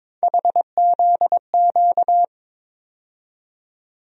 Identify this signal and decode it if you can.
Morse code, heard as a keyed tone, about 22 wpm, 700 Hz. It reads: HZQ
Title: USA Call Sign Postfixes - 3 Letters 22wpm